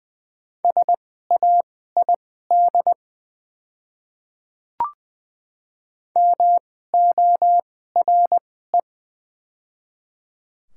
Morse code, heard as a keyed tone, about 20 wpm, spaced as for 14 wpm, 700 Hz. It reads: SAID MORE
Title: 2x Character Spacing - Top 500 Words - 20wpm